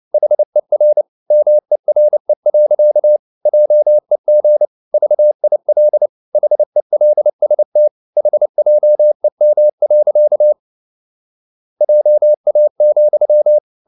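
Morse code, heard as a keyed tone, about 29 wpm, 600 Hz. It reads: HER MERE. JEG VIL HELST HJEM. JA,
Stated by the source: Fast (QRQ) Morse at 29 wpm, from CW med Gnister.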